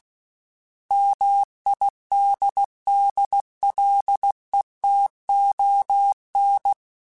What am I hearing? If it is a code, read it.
Morse code, heard as a keyed tone, about 16 wpm, 800 Hz. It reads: MIDDLETON
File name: fromSMs / FromSMs mp3s / Cities@15